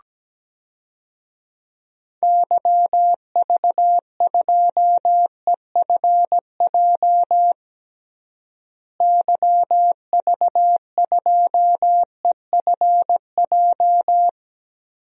Morse code, heard as a keyed tone, about 17 wpm, 700 Hz. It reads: YV2EFJ YV2EFJ
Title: DXCC-100 Call Signs - 6 Characters - Repeated 2x - 17wpm